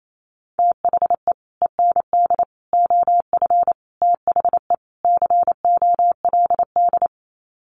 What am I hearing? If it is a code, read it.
Morse code, heard as a keyed tone, about 28 wpm, 700 Hz. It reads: THE END OF THE COLD
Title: Sets of 5 Words 28wpm